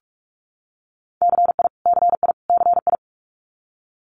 Morse code, heard as a keyed tone, about 46 wpm, 700 Hz. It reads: CI CI CI
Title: Multi-letter Phonemes Repeated 3x 45wpm